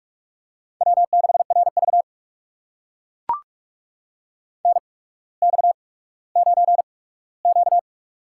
Morse code, heard as a keyed tone, about 44 wpm, spaced as for 19 wpm, 700 Hz. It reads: W6RV NX9Q
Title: USA Call Signs - 4 Characters - 8x Character Spacing - 45wpm